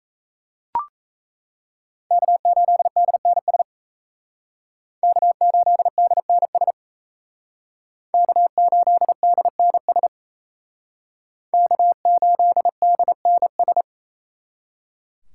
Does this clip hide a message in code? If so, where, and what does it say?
Yes: Morse code from 2.1 to 13.8 s, heard as a keyed tone, about 35 wpm, 700 Hz. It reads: K8DNS K8DNS K8DNS K8DNS